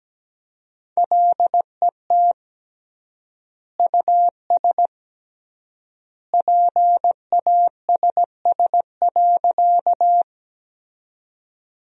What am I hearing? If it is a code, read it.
Morse code, heard as a keyed tone, about 17 wpm, 700 Hz. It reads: LET US PASS.
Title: 3x Word-Spacing - Sentences from Top 400 Words - 17wpm